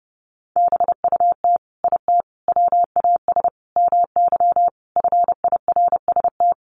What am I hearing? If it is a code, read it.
Morse code, heard as a keyed tone, about 28 wpm, 700 Hz. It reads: BUT IT WAS MY FIRST
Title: Sentences from Top 1300 Words 30wpm